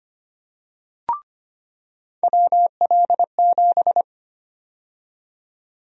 Morse code, heard as a keyed tone, about 25 wpm, 700 Hz. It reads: WL7
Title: USA Call Sign Prefixes - 25wpm